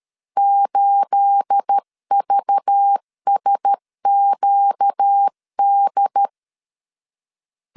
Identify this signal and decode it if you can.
Morse code, heard as a keyed tone, about 13 wpm, 800 Hz. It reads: 8VSQD
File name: LEZIONI / 9 28 12wpm 2351